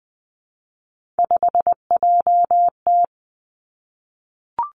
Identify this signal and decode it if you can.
Morse code, heard as a keyed tone, about 20 wpm, 700 Hz. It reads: 5JT
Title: Three Challenging Letter-Number Combinations 20wpm